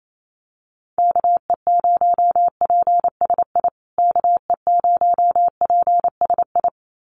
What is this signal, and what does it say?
Morse code, heard as a keyed tone, about 28 wpm, 700 Hz. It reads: KE0PSI KE0PSI
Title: USA Call Signs Repeated 2x 28wpm